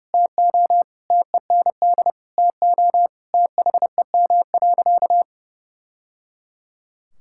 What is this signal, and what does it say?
Morse code, heard as a keyed tone, about 30 wpm, 700 Hz. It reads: TO TEND TO THEM.
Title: Sentences from Top 1000 Words - Review Part 5 - 30wpm